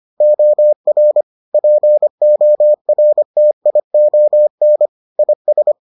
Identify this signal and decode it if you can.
Morse code, heard as a keyed tone, about 25 wpm, 600 Hz. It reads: OR PORTION IS